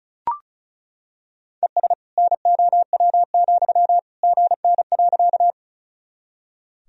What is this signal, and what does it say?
Morse code, heard as a keyed tone, about 34 wpm, 700 Hz. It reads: ES NOW, GN.
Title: Sentences from Top 300 Words and Abbreviations 35wpm